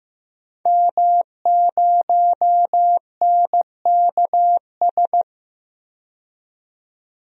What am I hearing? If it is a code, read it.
Morse code, heard as a keyed tone, about 15 wpm, 700 Hz. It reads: M0NKS
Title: DXCC-100 Call Signs - 5 Characters - 15wpm